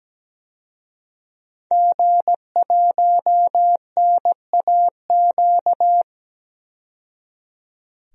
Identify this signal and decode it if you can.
Morse code, heard as a keyed tone, about 17 wpm, 700 Hz. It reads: G1NAQ
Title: DXCC-100 Call Signs - 5 Characters - 17wpm